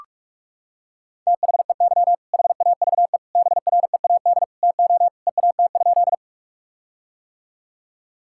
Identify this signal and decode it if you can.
Morse code, heard as a keyed tone, about 46 wpm, 700 Hz. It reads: THEY HAVE BREAD TO EAT?
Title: Sentences from Top 1600 Words 45wpm